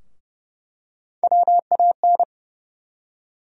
Morse code, heard as a keyed tone, about 30 wpm, 700 Hz. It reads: WAN
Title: USA Call Sign Postfixes - 3 Letters 30wpm